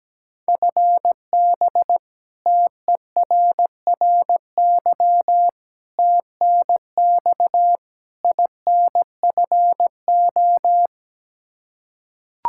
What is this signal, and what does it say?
Morse code, heard as a keyed tone, about 17 wpm, 700 Hz. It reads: FB TERRY TNX INFO
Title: Practice QSO - Exchange 3 - More Info - 17wpm